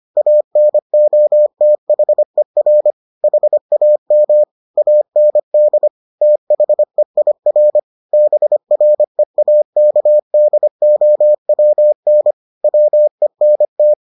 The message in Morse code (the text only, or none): ANOTHER HAM AND THEIR BREAKDOWN WENT